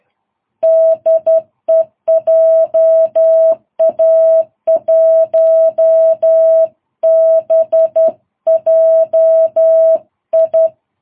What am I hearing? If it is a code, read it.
Morse code, heard as a keyed tone, about 11 wpm, 660 Hz. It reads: DE JA1BJI
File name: jh8cbh / 2022cw